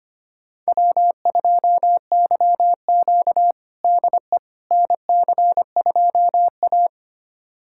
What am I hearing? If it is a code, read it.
Morse code, heard as a keyed tone, about 25 wpm, 700 Hz. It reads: W2YQ DE NC2A
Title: USA Callsign DE Callsign - 4 Characters - 25wpm